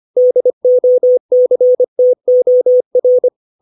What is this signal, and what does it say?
Morse code, heard as a keyed tone, about 25 wpm, 500 Hz. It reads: DOCTOR
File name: 632 words / 632 Words 25WPM